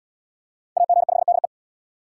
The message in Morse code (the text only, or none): AG5DE